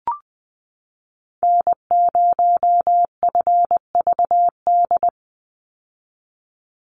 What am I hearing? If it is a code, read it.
Morse code, heard as a keyed tone, about 20 wpm, 700 Hz. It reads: N0FVD
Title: USA Call Signs 20wpm